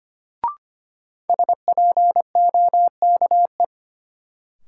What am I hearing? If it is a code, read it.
Morse code, heard as a keyed tone, about 25 wpm, 700 Hz. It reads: SPOKE
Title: Sentences from Top 1200 Words - New Word Drill Alt - 25wpm